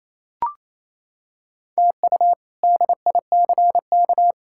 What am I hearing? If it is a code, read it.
Morse code, heard as a keyed tone, about 28 wpm, 700 Hz. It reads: TU DICK
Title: K1USN SST QSO Practice - 28wpm